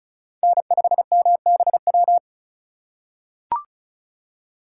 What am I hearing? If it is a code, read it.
Morse code, heard as a keyed tone, about 34 wpm, 700 Hz. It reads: N5MBW
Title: USA Call Signs - 5 Characters - 3x Character Spacing - 35wpm